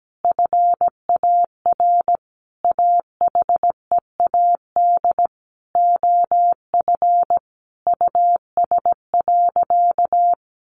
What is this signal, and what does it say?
Morse code, heard as a keyed tone, about 17 wpm, 700 Hz. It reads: FAR AHEAD OF US.
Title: Sentences from Top 1100 Words 17wpm